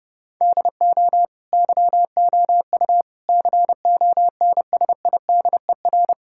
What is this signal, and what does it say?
Morse code, heard as a keyed tone, about 30 wpm, 700 Hz. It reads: DO YOU CONSIDER
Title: Sentences from Top 1000 Words - Review Part 6 - 30wpm